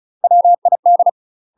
Morse code, heard as a keyed tone, about 37 wpm, 700 Hz. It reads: WID
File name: UK MASTER QSOs TRAINING / 35WPM / UK MASTER QSOs 0030